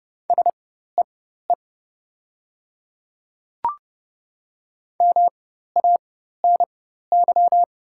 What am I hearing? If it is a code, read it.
Morse code, heard as a keyed tone, about 29 wpm, spaced as for 14 wpm, 700 Hz. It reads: SEE MANY
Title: 4x Character Spacing - Top 100 Words - 30wpm